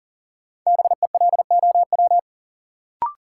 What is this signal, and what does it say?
Morse code, heard as a keyed tone, about 40 wpm, 700 Hz. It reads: BELOW
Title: Sentences from Top 1000 Words - Review Part 5 - New Word Drill - 40wpm